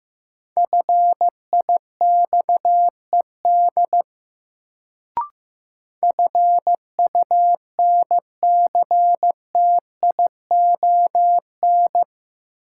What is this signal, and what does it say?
Morse code, heard as a keyed tone, about 15 wpm, 700 Hz. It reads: FIXED FUNCTION
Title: Sentences from Top 1100 Words - New Word Drill - 15wpm